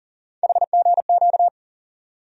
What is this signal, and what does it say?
Morse code, heard as a keyed tone, about 38 wpm, 700 Hz. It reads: HGQ
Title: USA Call Sign Postfixes - 3 Letters 40wpm